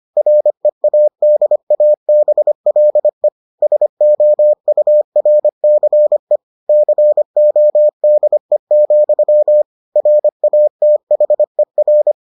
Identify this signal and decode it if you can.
Morse code, heard as a keyed tone, about 25 wpm, 600 Hz. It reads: READABLE SOURCE CODE, RATHER